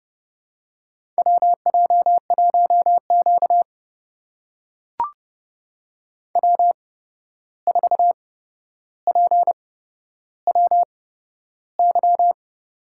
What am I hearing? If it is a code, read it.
Morse code, heard as a keyed tone, about 31 wpm, spaced as for 12 wpm, 700 Hz. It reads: WJ1Q W4PWY
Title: USA Call Signs - 8x Character Spacing - 30wpm